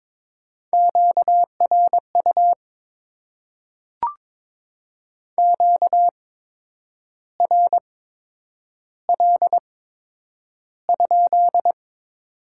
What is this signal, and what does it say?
Morse code, heard as a keyed tone, about 22 wpm, spaced as for 8 wpm, 700 Hz. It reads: QRU QRL?
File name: Top 90 QSO Elements - 8x Character Spacing - 22wpm